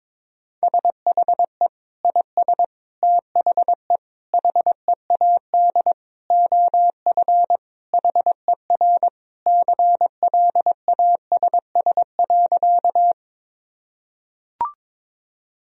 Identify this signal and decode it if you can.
Morse code, heard as a keyed tone, about 22 wpm, 700 Hz. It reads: SHE IS THE HEAD OF HER CLASS.
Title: Sentences from Top 1000 Words - Review Part 2 - 22wpm